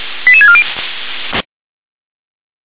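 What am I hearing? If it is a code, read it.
Five-tone selective call (ZVEI), heard as quick runs of five tones, tones of about 70 ms. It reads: 88640
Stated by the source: Five-tone ZVEI selective call 88088 received by SDR.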